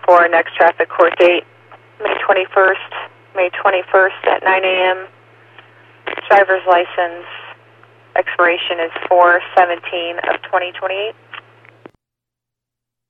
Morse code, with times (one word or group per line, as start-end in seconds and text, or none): none